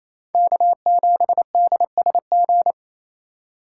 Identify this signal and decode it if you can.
Morse code, heard as a keyed tone, about 28 wpm, 700 Hz. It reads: K7DSG
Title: USA Call Signs - 5 Characters - 4x Character Spacing - 28wpm